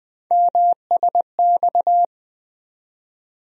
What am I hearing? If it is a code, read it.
Morse code, heard as a keyed tone, about 20 wpm, 700 Hz. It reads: MSX
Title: USA Call Sign Postfixes - 3 Letters 20wpm